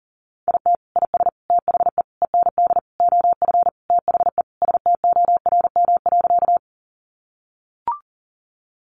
Morse code, heard as a keyed tone, about 40 wpm, 700 Hz. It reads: IT IS THE END OF THE STORM.
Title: Sentences from Top 1500 Words 40wpm